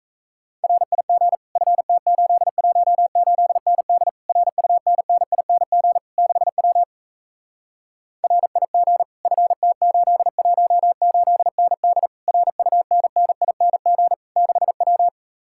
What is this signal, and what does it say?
Morse code, heard as a keyed tone, about 38 wpm, 700 Hz. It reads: RIG FT818ND RUNNING 6W RIG FT818ND RUNNING 6W